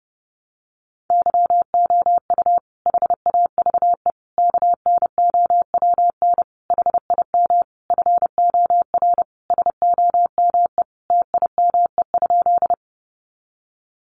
Morse code, heard as a keyed tone, about 30 wpm, 700 Hz. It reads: YOU HAVE KNOWN HIM FOR SOME TIME?
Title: Sentences from Top 600 Words 30wpm